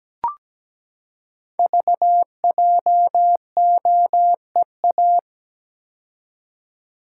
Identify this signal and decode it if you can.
Morse code, heard as a keyed tone, about 17 wpm, 700 Hz. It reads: VJOEA
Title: Code Groups - 5 Letters - 17wpm